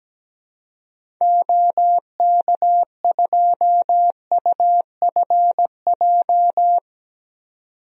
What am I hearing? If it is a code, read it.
Morse code, heard as a keyed tone, about 17 wpm, 700 Hz. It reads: OK2UFJ
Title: DXCC-100 Call Signs - 6 Characters - 17wpm